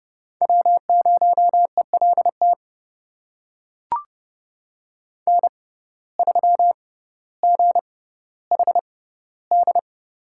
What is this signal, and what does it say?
Morse code, heard as a keyed tone, about 32 wpm, spaced as for 16 wpm, 700 Hz. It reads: W0ELT N3GHD
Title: USA Call Signs - 5 Characters - 6x Character Spacing - 30wpm